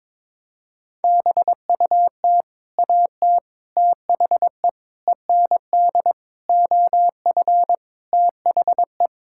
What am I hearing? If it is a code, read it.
Morse code, heard as a keyed tone, about 21 wpm, 700 Hz. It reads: BUT AT THE END OF THE
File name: practice / 22wpm / Sets of 8 Words - 22wpm